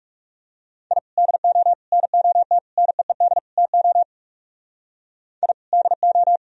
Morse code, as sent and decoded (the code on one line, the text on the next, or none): .. / -.. --- / -. --- - / -. . -.. / - --- / .. / -.. ---
I DO NOT NEED TO I DO